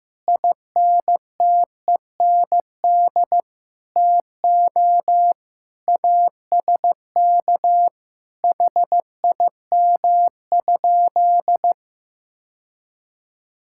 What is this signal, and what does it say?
Morse code, heard as a keyed tone, about 15 wpm, 700 Hz. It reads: INTEND TO ASK HIM?
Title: Sentences from Top 1600 Words 15wpm